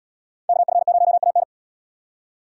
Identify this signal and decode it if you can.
Morse code, heard as a keyed tone, about 71 wpm, 700 Hz. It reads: KF0AN